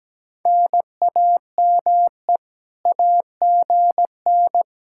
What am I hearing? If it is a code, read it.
Morse code, heard as a keyed tone, about 17 wpm, 700 Hz. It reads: NAME AGN
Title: Speed-Racing Common QSO Phrases - 17wpm